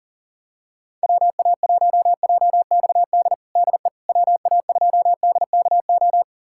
Morse code, heard as a keyed tone, about 37 wpm, 700 Hz. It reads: WA1JXD DE WA2DKO
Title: USA Callsign DE Callsign - 6 Characters - 40wpm